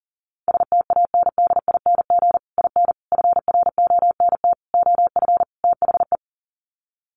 Morse code, heard as a keyed tone, about 37 wpm, 700 Hz. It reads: STANDING IN FRONT OF THE
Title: Sets of 5 Words Encore2 40wpm